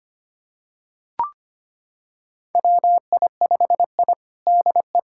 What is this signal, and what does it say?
Morse code, heard as a keyed tone, about 25 wpm, 700 Hz. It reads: WI5I DE
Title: USA Callsign DE Callsign - 4 Characters - 25wpm